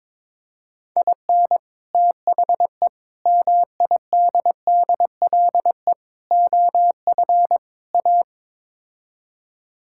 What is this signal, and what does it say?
Morse code, heard as a keyed tone, about 22 wpm, 700 Hz. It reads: IN THE MIDDLE OF A
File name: practice / 22wpm / Build A Phrase - 5-Word Phrases - 22wpm